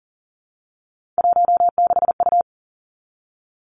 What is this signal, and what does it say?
Morse code, heard as a keyed tone, about 38 wpm, 690 Hz. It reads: 16U
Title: Three Challenging Letter-Number Combinations 40wpm